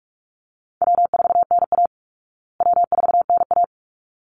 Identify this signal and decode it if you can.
Morse code, heard as a keyed tone, about 44 wpm, 700 Hz. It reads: W4NA W4NA
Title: Easy Call Signs Repeated 2x - 45wpm